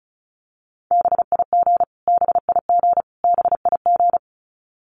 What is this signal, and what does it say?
Morse code, heard as a keyed tone, about 34 wpm, 700 Hz. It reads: BIG BIG BIG
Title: Top 500 Words Repeated 35wpm